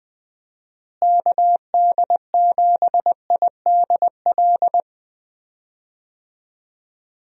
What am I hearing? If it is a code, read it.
Morse code, heard as a keyed tone, about 20 wpm, 700 Hz. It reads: KD7IDL
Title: USA Call Signs - 6 Characters - Repeated 2x - 20wpm